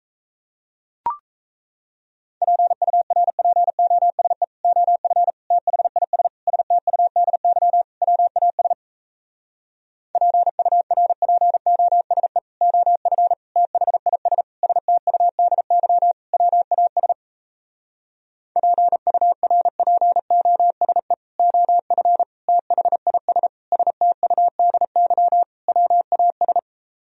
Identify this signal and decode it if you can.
Morse code, heard as a keyed tone, about 37 wpm, 700 Hz. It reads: PURPOSE OF THIS STUDY WAS PURPOSE OF THIS STUDY WAS PURPOSE OF THIS STUDY WAS